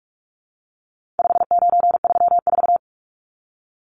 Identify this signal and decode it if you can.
Morse code, heard as a keyed tone, about 44 wpm, 700 Hz. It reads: 5934
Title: Numbers - 4-Digit 45wpm